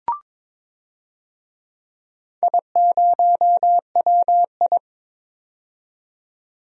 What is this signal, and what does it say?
Morse code, heard as a keyed tone, about 22 wpm, 700 Hz. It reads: I0WI
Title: DXCC-100 Call Signs - 4 Characters - 22wpm